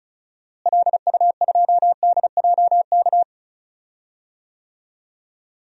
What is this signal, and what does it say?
Morse code, heard as a keyed tone, about 34 wpm, 700 Hz. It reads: LU2DJK
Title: DXCC-100 Call Signs - 6 Characters - Repeated 3x - 35wpm